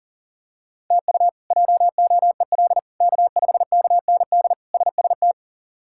Morse code, heard as a keyed tone, about 40 wpm, 690 Hz. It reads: TU JOEL K5KND SST
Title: K1USN SST QSO Practice - 40wpm